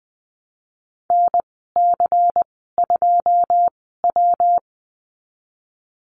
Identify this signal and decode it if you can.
Morse code, heard as a keyed tone, about 20 wpm, spaced as for 15 wpm, 700 Hz. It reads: NC2W
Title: 128 Lesson 21g - Number 6 - Call Signs 20wpm